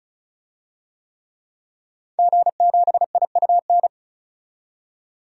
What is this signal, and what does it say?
Morse code, heard as a keyed tone, about 36 wpm, 700 Hz. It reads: G7IUN
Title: DXCC-100 Call Signs - 5 Characters - 35wpm